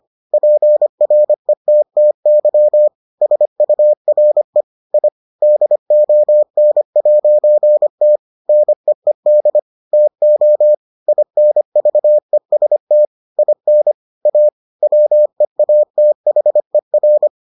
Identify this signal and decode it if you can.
Morse code, heard as a keyed tone, about 25 wpm, 600 Hz. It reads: PRETTY SURE I DON'T NEED TO INVEST IN A WEATHER